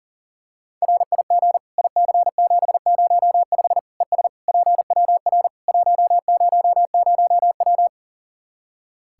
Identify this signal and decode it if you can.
Morse code, heard as a keyed tone, about 40 wpm, 700 Hz. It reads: RIG IC705 ES PWR 100W
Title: Practice QSO - Exchange 2 - Info - 40wpm